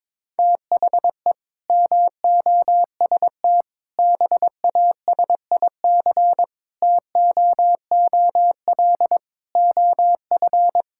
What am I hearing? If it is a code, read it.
Morse code, heard as a keyed tone, about 22 wpm, 700 Hz. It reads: THE MOST BASIC TOOL OF
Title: Sentences from Top 1000 Words - Review Part 6 - 22wpm